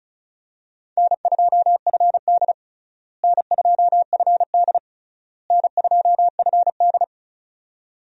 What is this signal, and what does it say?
Morse code, heard as a keyed tone, about 34 wpm, 700 Hz. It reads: N2FD N2FD N2FD
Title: USA Call Signs - 4 Characters - Repeated 3x - 35wpm